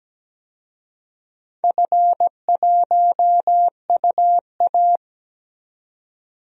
Morse code, heard as a keyed tone, about 17 wpm, 700 Hz. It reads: F1UA
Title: DXCC-100 Call Signs - 4 Characters - 17wpm